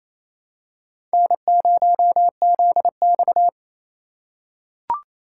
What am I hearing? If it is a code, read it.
Morse code, heard as a keyed tone, about 27 wpm, 700 Hz. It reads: N0ZX
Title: USA Call Signs - 4 Characters - 6x Character Spacing - 28wpm